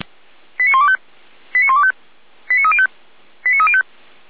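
Five-tone selective call (ZVEI), heard as quick runs of five tones, tones of about 80 ms. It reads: 89125 89125 89305 89305
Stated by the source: ZVEI five-tone call 89277 received by SDR, archived by feuerwehr-orlamuende.org fire brigade.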